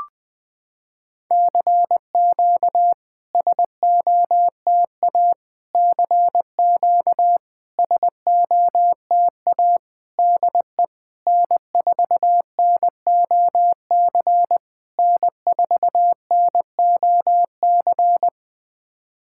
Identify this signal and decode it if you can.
Morse code, heard as a keyed tone, about 21 wpm, 700 Hz. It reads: CQ SOTA CQ SOTA DE N4NOC N4NOC
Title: SOTA QSO Practice 20wpm